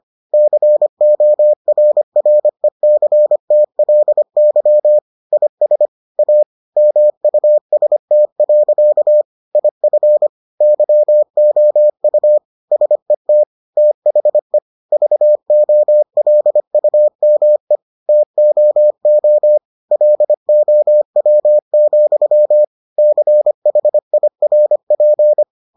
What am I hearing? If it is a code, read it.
Morse code, heard as a keyed tone, about 25 wpm, 600 Hz. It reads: CORRECTLY IS A MUST. IF YOU SET THE VOLUME TOO LOW, CHIRP